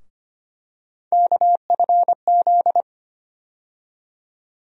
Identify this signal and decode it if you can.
Morse code, heard as a keyed tone, about 25 wpm, 700 Hz. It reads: KFZ
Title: USA Call Sign Postfixes - 3 Letters 25wpm